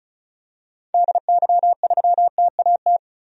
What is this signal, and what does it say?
Morse code, heard as a keyed tone, about 36 wpm, 700 Hz. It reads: DY3TAT
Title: DXCC-100 Call Signs - 6 Characters - 35wpm